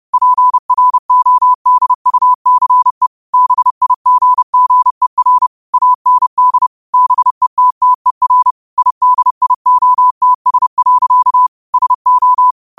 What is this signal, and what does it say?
Morse code, heard as a keyed tone, about 30 wpm, 1000 Hz. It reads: PRODUCE BIGGER AND BETTER IDIOTS. SO